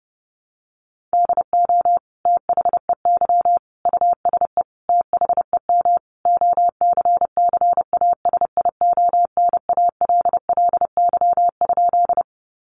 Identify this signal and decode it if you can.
Morse code, heard as a keyed tone, about 30 wpm, 700 Hz. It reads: DO THEY USE THEM OCCASIONALLY?
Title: Sentences from Top 1300 Words 30wpm